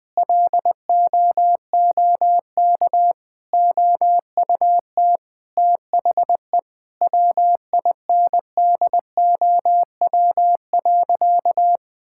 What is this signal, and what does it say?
Morse code, heard as a keyed tone, about 20 wpm, 700 Hz. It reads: LOOK OUT THE WINDOW.
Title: Sentences from Top 1600 Words 20wpm